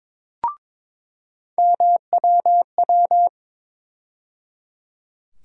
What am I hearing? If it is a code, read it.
Morse code, heard as a keyed tone, about 22 wpm, 700 Hz. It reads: MWW
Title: USA Call Sign Postfixes - 3 Letters 22wpm